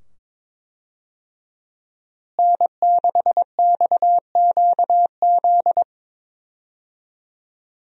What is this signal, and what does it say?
Morse code, heard as a keyed tone, about 22 wpm, 700 Hz. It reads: N6XQZ
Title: USA Call Signs - 5 Characters - Repeated 2x - 22wpm